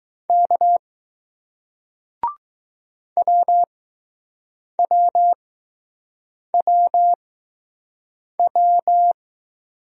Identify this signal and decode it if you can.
Morse code, heard as a keyed tone, about 20 wpm, spaced as for 6 wpm, 700 Hz. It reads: K WWWW